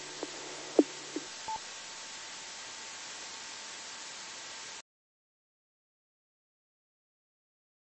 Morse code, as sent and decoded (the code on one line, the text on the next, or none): none